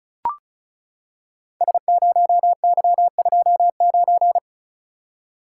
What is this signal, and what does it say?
Morse code, heard as a keyed tone, about 36 wpm, 700 Hz. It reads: S0Y29
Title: Code Groups - 5 Letter:Numbers - 35wpm